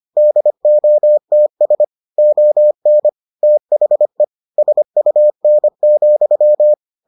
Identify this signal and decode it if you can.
Morse code, heard as a keyed tone, about 25 wpm, 600 Hz. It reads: DOTS ON THE SUN,